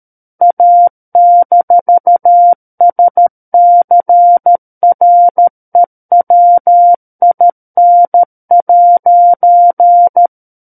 Morse code, heard as a keyed tone, about 13 wpm, 700 Hz. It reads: A-SCREWIN'
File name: Book One / 04-Chapter 4 - 13 WPM